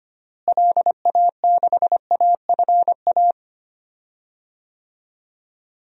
Morse code, heard as a keyed tone, about 25 wpm, 700 Hz. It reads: LA6AFA
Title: DXCC-100 Call Signs - 6 Characters - Repeated 2x - 25wpm